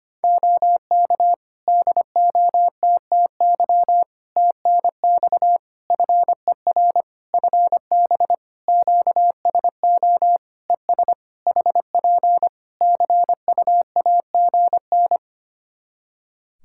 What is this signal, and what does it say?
Morse code, heard as a keyed tone, about 25 wpm, 700 Hz. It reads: OK DOTTY TNX FER FB QSO ES HP CUAGN
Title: K1USN SST QSO Practice 25wpm